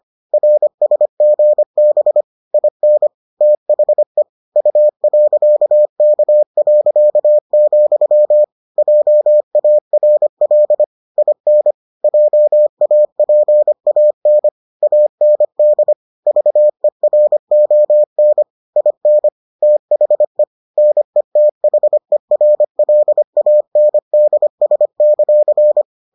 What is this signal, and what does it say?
Morse code, heard as a keyed tone, about 25 wpm, 600 Hz. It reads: RSGB IN THE U.K., JARL IN JAPAN AND VERON IN THE NETHERLANDS;